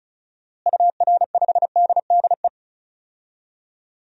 Morse code, heard as a keyed tone, about 36 wpm, 700 Hz. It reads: UR5DDE
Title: DXCC-100 Call Signs - 6 Characters - Repeated 3x - 35wpm